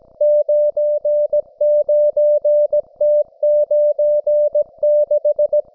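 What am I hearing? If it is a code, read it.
Morse code, heard as a keyed tone, about 17 wpm, 590 Hz. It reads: 99T96
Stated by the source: Russian military Morse station, M32, RDL, on 5268 kHz.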